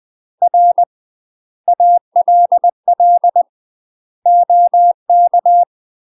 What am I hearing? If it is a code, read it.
Morse code, heard as a keyed tone, about 20 wpm, 700 Hz. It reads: R ALL OK
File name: UK MASTER QSOs TRAINING / 20WPM / UK MASTER QSOs 0071